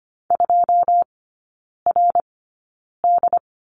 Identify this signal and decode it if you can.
Morse code, heard as a keyed tone, about 25 wpm, spaced as for 9 wpm, 700 Hz. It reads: 2RD
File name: Lesson 14e - Number 2 - Three Characters 25wpm